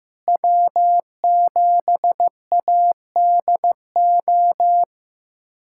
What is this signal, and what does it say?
Morse code, heard as a keyed tone, about 15 wpm, 700 Hz. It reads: W7ADO
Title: USA Callsign DE Callsign - 5 Characters - 15wpm